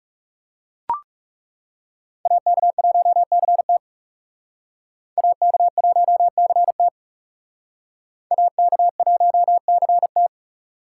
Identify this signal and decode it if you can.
Morse code, heard as a keyed tone, about 37 wpm, 700 Hz. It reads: AK1CT AK1CT AK1CT
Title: Speed Racing - USA Call Signs 30wpm